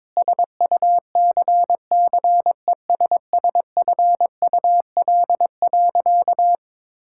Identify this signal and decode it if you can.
Morse code, heard as a keyed tone, about 22 wpm, 700 Hz. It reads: SUCCESSFUL.
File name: Sentences with 7-Word Phrases - 22wpm